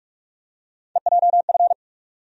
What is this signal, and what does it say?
Morse code, heard as a keyed tone, about 46 wpm, 700 Hz. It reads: EJF
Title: USA Call Sign Postfixes - 3 Letters 45wpm